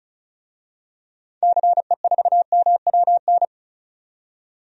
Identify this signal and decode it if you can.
Morse code, heard as a keyed tone, about 36 wpm, 700 Hz. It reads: CE4MWN
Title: DXCC-100 Call Signs - 6 Characters - 35wpm